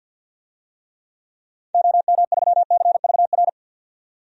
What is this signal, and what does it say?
Morse code, heard as a keyed tone, about 49 wpm, 700 Hz. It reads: OM3CVR
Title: DXCC-100 Call Signs - 6 Characters - 50wpm